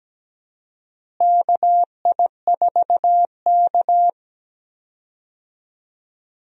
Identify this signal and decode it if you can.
Morse code, heard as a keyed tone, about 17 wpm, 700 Hz. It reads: KI4K
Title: USA Call Signs - 4 Characters - Repeated 3x - 17wpm